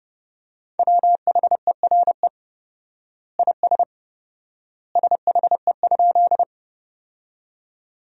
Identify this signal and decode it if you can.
Morse code, heard as a keyed tone, about 30 wpm, 700 Hz. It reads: WHERE IS SHE?
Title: 4x Word-Spacing - Sentences from Top 200 Words - 30wpm